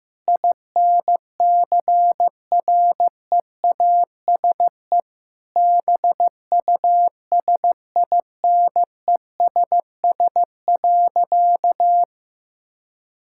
Morse code, heard as a keyed tone, about 15 wpm, 700 Hz. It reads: INCREASE BUSINESS.
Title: Sentences from Top 1000 Words - Review Part 6 - 15wpm